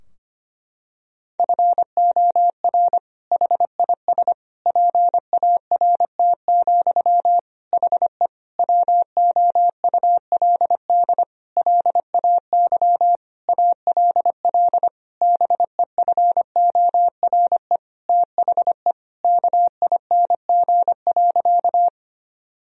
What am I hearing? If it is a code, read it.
Morse code, heard as a keyed tone, about 25 wpm, 700 Hz. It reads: FOR HIS PART, HE WOULD LAY ALL BEFORE THE KING.
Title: Sentences from Top 1000 Words - Review Part 2 - 25wpm